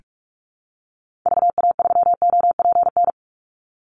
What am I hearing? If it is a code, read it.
Morse code, heard as a keyed tone, about 44 wpm, 700 Hz. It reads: VA3OPN
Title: Call Signs Repeated 3x - 45wpm